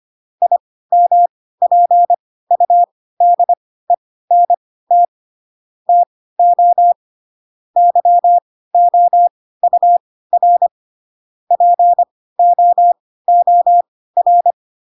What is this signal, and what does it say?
Morse code, heard as a keyed tone, about 25 wpm, spaced as for 16 wpm, 700 Hz. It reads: IMPUDENT TO YOUR POOR
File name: Pinocchio / Pinocchio ch3 10 25 ik1hge